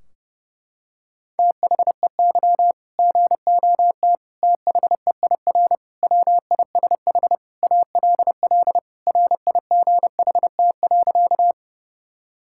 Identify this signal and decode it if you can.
Morse code, heard as a keyed tone, about 32 wpm, 700 Hz. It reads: THEY GOT THEIR WISH ALL RIGHT.
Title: Sentences from Top 1600 Words 30wpm